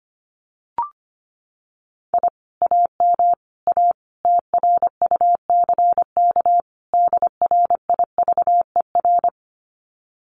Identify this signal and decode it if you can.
Morse code, heard as a keyed tone, about 25 wpm, 700 Hz. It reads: I AM A TRUCK DRIVER
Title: Common QSO Phrases Encore 25wpm